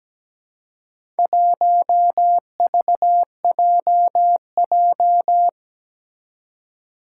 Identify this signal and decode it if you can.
Morse code, heard as a keyed tone, about 17 wpm, 700 Hz. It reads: 1VJJ
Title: Four Challenging Letter-Number Combinations 17wpm